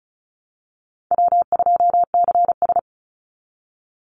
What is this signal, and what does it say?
Morse code, heard as a keyed tone, about 34 wpm, 700 Hz. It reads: W2CS
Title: Call Signs Repeated 3x - 35wpm